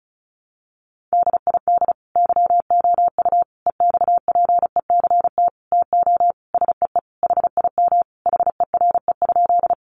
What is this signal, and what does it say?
Morse code, heard as a keyed tone, about 34 wpm, 700 Hz. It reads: DID YOU EXPECT TO SEE HIM HERE?
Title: Sentences from Top 1100 Words 35wpm